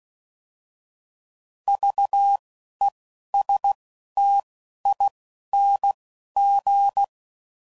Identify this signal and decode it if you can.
Morse code, heard as a keyed tone, about 16 wpm, 800 Hz. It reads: V E S T I N G
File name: Prefixes / By First Letter / Vs